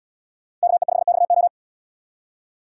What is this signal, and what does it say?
Morse code, heard as a keyed tone, about 77 wpm, 700 Hz. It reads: KE5XO